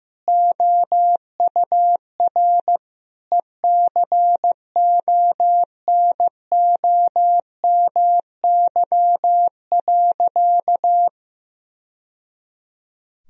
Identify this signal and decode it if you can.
Morse code, heard as a keyed tone, about 16 wpm, 700 Hz. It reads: OUR ECONOMY.
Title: Sentences from Top 1000 Words - Review Part 3 - 15wpm